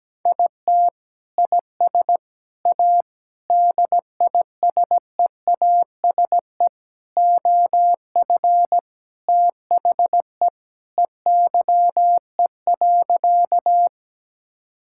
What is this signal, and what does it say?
Morse code, heard as a keyed tone, about 17 wpm, 700 Hz. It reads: IT IS A DISEASE OF THE EYE.
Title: Sentences from Top 1000 Words - Review Part 6 - 17wpm